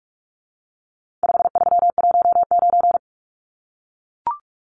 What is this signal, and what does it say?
Morse code, heard as a keyed tone, about 44 wpm, 700 Hz. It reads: 5319
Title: Numbers - 4-Digit 45wpm